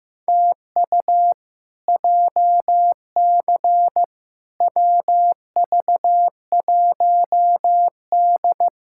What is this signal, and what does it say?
Morse code, heard as a keyed tone, about 16 wpm, 700 Hz. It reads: TU JC WV1D
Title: K1USN SST QSO Practice - 15wpm